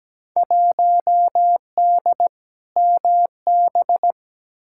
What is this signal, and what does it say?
Morse code, heard as a keyed tone, about 17 wpm, 700 Hz. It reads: 1D MB
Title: Field Day Exchanges Encore - 17wpm